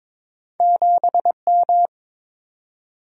Morse code, heard as a keyed tone, about 22 wpm, 700 Hz. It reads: 7M
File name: practice / 22wpm / Two Letter-Number Combinations - 22wpm